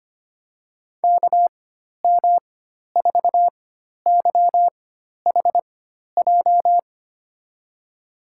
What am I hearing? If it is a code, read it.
Morse code, heard as a keyed tone, about 25 wpm, spaced as for 12 wpm, 700 Hz. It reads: KM4YHJ
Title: USA Call Signs - 6 Characters - 4x Character Spacing - 25wpm